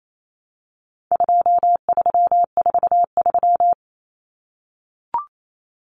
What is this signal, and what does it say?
Morse code, heard as a keyed tone, about 28 wpm, 700 Hz. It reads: 2343
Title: Numbers - 4-Digit 28wpm